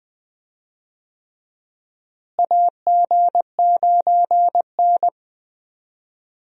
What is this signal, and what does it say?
Morse code, heard as a keyed tone, about 20 wpm, 700 Hz. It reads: AG9N
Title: USA Call Signs - 4 Characters - Repeated 3x - 20wpm